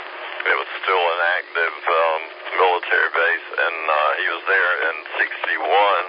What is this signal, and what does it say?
Emergency Alert System (SAME) tones, heard as harsh buzzing bursts, with speech. It is heard from 1.2 to 5.0 s.